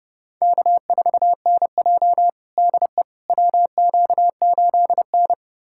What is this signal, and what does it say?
Morse code, heard as a keyed tone, about 32 wpm, 700 Hz. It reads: K4NJ DE WQ8N
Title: USA Callsign DE Callsign - 4 Characters - 30wpm